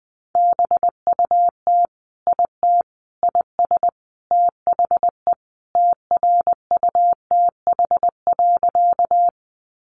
Morse code, heard as a keyed tone, about 20 wpm, 700 Hz. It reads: BUT IT IS THE TRUTH.
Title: Sentences from Top 800 Words 20wpm